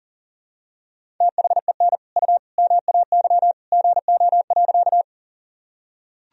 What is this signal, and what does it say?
Morse code, heard as a keyed tone, about 40 wpm, 700 Hz. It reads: THEN U MAY GO.
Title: Sentences from Top 100 Words and Abbreviations 40wpm